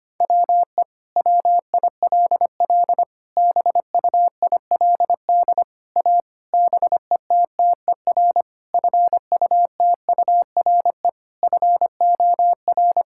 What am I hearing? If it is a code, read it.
Morse code, heard as a keyed tone, about 25 wpm, 700 Hz. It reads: WE WILL BUILD A BETTER FUTURE FOR
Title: Sentences from Top 1000 Words - Review Part 6 - 25wpm